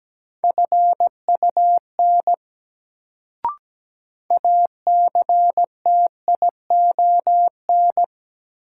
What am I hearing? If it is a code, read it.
Morse code, heard as a keyed tone, about 17 wpm, 700 Hz. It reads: FUN ACTION
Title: Sentences from Top 700 Words - New Word Drill - 17wpm